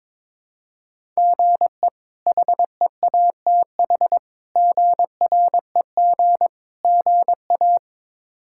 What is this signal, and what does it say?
Morse code, heard as a keyed tone, about 22 wpm, 700 Hz. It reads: GE HEATH GREG GA
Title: K1USN SST QSO Practice - 22wpm